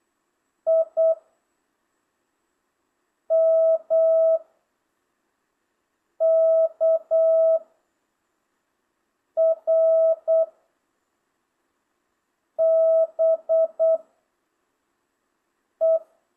Morse code, heard as a keyed tone, about 8 wpm, 650 Hz. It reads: I M K R B E